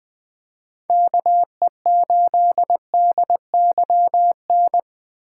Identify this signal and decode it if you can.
Morse code, heard as a keyed tone, about 20 wpm, 700 Hz. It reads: KE8DYN
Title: USA Call Signs - 6 Characters - 2x Character Spacing - 20wpm